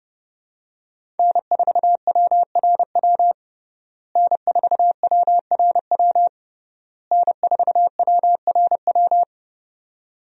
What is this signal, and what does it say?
Morse code, heard as a keyed tone, about 32 wpm, 700 Hz. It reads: N4WRW N4WRW N4WRW
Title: USA Call Signs - 5 Characters - Repeated 3x - 30wpm